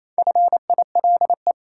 Morse code, heard as a keyed tone, about 28 wpm, 700 Hz. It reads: FILE
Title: Sentences from Top 1000 Words - Review Part 4 - New Word Drill - 28wpm